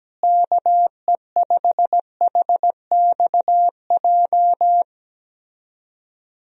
Morse code, heard as a keyed tone, about 17 wpm, 700 Hz. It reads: KE5HXJ
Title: USA Call Signs - 6 Characters - Repeated 2x - 17wpm